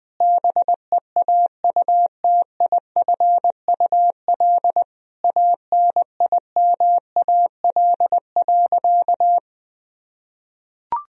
Morse code, heard as a keyed tone, about 20 wpm, 700 Hz. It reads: BEAUTIFUL ANIMAL.
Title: Sentences from Top 1000 Words - Review Part 1 - 20wpm